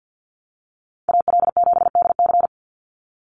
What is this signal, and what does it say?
Morse code, heard as a keyed tone, about 50 wpm, 700 Hz. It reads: AL7DC
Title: USA Call Signs 50wpm